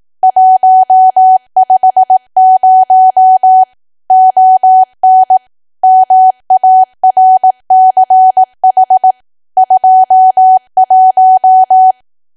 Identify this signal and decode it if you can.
Morse code, heard as a keyed tone, about 18 wpm, 750 Hz. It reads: 150 ON MARCH 21
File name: cwa / W1AW-Level2 / 18WPM